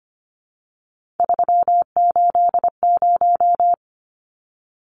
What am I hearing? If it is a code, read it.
Morse code, heard as a keyed tone, about 25 wpm, 700 Hz. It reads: 380
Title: Numbers - 3-Digit 25wpm